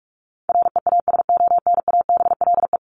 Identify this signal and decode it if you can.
Morse code, heard as a keyed tone, about 44 wpm, 700 Hz. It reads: REASONABLE
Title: Top 1501-1750 Words 45wpm